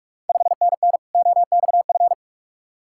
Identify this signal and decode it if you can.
Morse code, heard as a keyed tone, about 44 wpm, 690 Hz. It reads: 5NN OXF
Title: Ontario QSO Party Exchange 45wpm